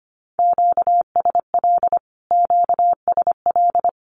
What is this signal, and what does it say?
Morse code, heard as a keyed tone, about 25 wpm, 700 Hz. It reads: QSL QSL
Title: Q-Codes Repeated 3x 25wpm